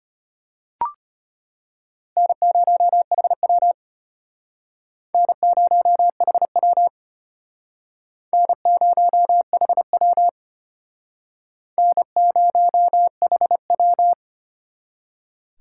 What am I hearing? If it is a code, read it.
Morse code, heard as a keyed tone, about 32 wpm, 700 Hz. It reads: N0HW N0HW N0HW N0HW